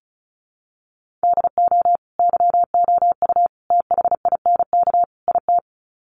Morse code, heard as a keyed tone, about 34 wpm, 700 Hz. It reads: DO YOU THINK IT
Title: Sets of 4 Words 35wpm